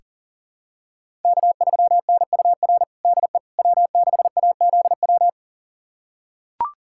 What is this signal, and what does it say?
Morse code, heard as a keyed tone, about 40 wpm, 700 Hz. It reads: K3NUR DE W6AZW
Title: USA Callsign DE Callsign - 5 Characters - 40wpm